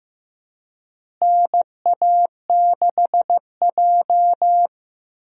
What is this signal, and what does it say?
Morse code, heard as a keyed tone, about 16 wpm, 700 Hz. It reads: NA6J
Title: USA Call Signs - 4 Characters - Repeated 3x - 15wpm